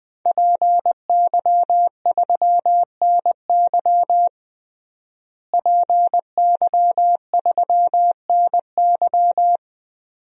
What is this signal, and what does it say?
Morse code, heard as a keyed tone, about 20 wpm, 700 Hz. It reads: PY3NY PY3NY
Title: DXCC-100 Call Signs - 5 Characters - Repeated 2x - 20wpm